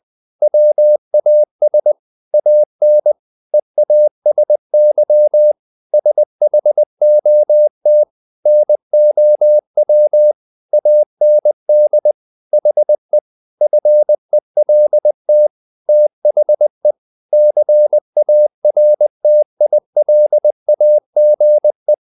Morse code, heard as a keyed tone, about 20 wpm, 600 Hz. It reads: WAS AN EASY SHOT NOW AND HE FELT THE CARTILAGE
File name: the old man and the sea part 2 20wpm